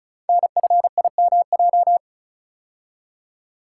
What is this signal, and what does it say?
Morse code, heard as a keyed tone, about 36 wpm, 690 Hz. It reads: NFIMJ
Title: Code Groups - 5 Letters - 35wpm